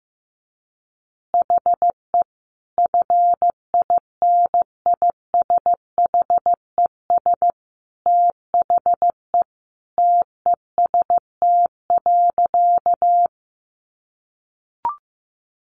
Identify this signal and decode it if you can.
Morse code, heard as a keyed tone, about 15 wpm, 700 Hz. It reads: HE FINISHES THE TEST.
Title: Additional Easy Phrases 15wpm